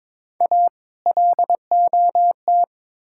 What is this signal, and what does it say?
Morse code, heard as a keyed tone, about 22 wpm, 700 Hz. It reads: A LOT
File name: practice / 22wpm / Build A Phrase - 5-Word Phrases - 22wpm